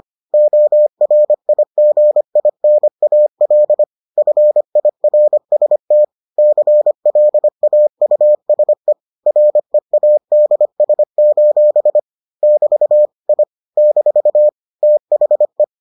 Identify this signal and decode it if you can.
Morse code, heard as a keyed tone, about 25 wpm, 600 Hz. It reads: ORIGINAL FIRST CLAUSE READS: <BT> I - THE